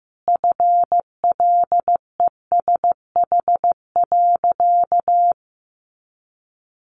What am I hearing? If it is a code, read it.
Morse code, heard as a keyed tone, about 15 wpm, 700 Hz. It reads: FLESH.
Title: A Sound Of Thunder 15wpm